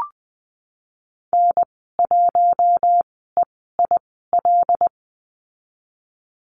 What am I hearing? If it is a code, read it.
Morse code, heard as a keyed tone, about 20 wpm, spaced as for 15 wpm, 700 Hz. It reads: N1EIL
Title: 076 Lesson 13g - Letter L - Call Signs 20wpm